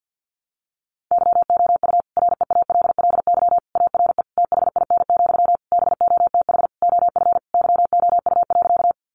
Morse code, heard as a keyed tone, about 50 wpm, 700 Hz. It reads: YOU REALLY ARE THIN, BOTH OF YOU.
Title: Sentences from Top 1200 Words 50wpm